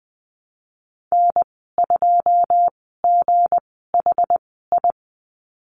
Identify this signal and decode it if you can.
Morse code, heard as a keyed tone, about 20 wpm, spaced as for 15 wpm, 700 Hz. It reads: N2GHI
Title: 213 Lesson 34g - Number 8 - Call Signs 20wpm